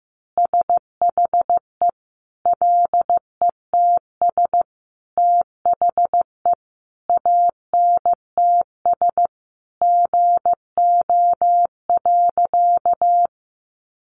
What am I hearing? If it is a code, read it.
Morse code, heard as a keyed tone, about 15 wpm, 700 Hz. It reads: SHE LETS THE ANTS GO.
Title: Easy Phrases 15wpm